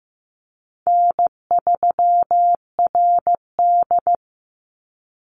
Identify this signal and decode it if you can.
Morse code, heard as a keyed tone, about 15 wpm, 700 Hz. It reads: N3RD
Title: Easy Call Signs - 15wpm